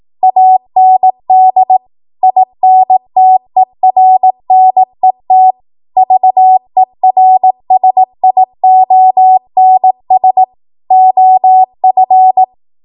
Morse code, heard as a keyed tone, about 18 wpm, 750 Hz. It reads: AND INTERNET VERSIONS OF
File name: Morse / 18WPM